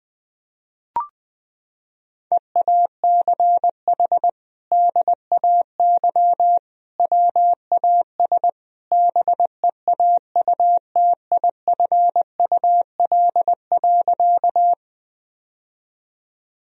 Morse code, heard as a keyed tone, about 20 wpm, 700 Hz. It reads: EACH DAY WAS BEAUTIFUL.
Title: Sentences from Top 1000 Words - Review Part 1 - 20wpm